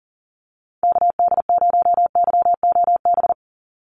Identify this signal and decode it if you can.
Morse code, heard as a keyed tone, about 40 wpm, 700 Hz. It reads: KD0YOB
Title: NA SOTA Activator Callsigns 40wpm